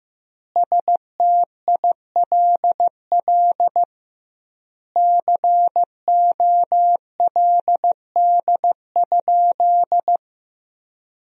Morse code, heard as a keyed tone, about 15 wpm, 700 Hz. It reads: STILL COLD?
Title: 2x Word-Spacing - Sentences from Top 400 Words - 15wpm